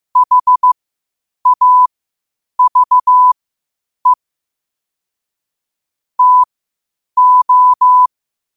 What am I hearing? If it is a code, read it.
Morse code, heard as a keyed tone, about 15 wpm, spaced as for 8 wpm, 1000 Hz. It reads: HAVE TO
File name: qotdfiles / 8wpm